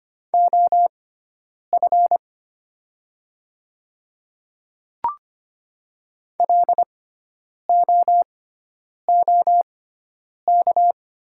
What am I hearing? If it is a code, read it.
Morse code, heard as a keyed tone, about 25 wpm, spaced as for 9 wpm, 700 Hz. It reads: OF LOOK
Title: 6x Character Spacing - Top 100 Words - 25wpm